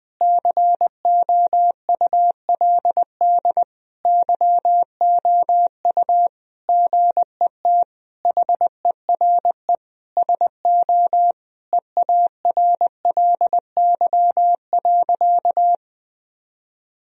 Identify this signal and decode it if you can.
Morse code, heard as a keyed tone, about 21 wpm, 700 Hz. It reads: COULD YOU GET HERE SO EARLY.
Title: Sentences from Top 1000 Words - Review Part 2 - 20wpm